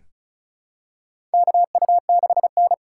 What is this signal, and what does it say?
Morse code, heard as a keyed tone, about 34 wpm, 700 Hz. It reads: KU6N
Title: USA Call Signs - 4 Characters - 2x Character Spacing - 35wpm